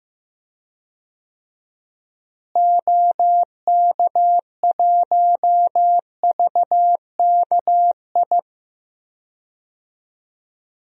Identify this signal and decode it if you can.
Morse code, heard as a keyed tone, about 15 wpm, 700 Hz. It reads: OK1VKI
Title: DXCC-100 Call Signs - 6 Characters - Repeated 2x - 15wpm